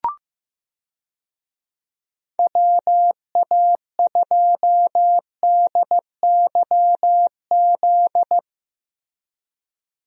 Morse code, heard as a keyed tone, about 15 wpm, 700 Hz. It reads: WA2DYZ
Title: USA Call Signs - 6 Characters - Repeated 3x - 15wpm